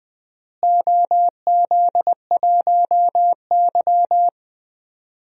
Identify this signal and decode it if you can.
Morse code, heard as a keyed tone, about 20 wpm, 700 Hz. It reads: OZ1Y
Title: DXCC-100 Call Signs - 4 Characters - Repeated 2x - 20wpm